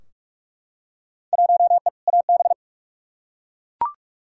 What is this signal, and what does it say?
Morse code, heard as a keyed tone, about 41 wpm, 700 Hz. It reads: 1E AB
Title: Field Day Exchanges - 45wpm